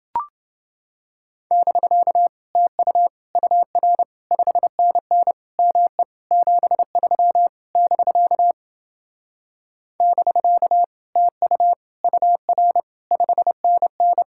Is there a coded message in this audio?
Yes: Morse code from 1.5 to 14.2 s, heard as a keyed tone, about 27 wpm, 700 Hz. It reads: <BK> TU UR 5NN ME 73 <BK> <BK> TU UR 5NN